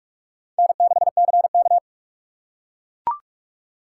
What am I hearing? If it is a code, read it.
Morse code, heard as a keyed tone, about 42 wpm, 700 Hz. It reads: N6CK